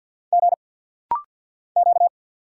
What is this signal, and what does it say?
Morse code, heard as a keyed tone, about 50 wpm, 700 Hz. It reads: G Q
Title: Single Letter-Number - Rapid-Fire 50wpm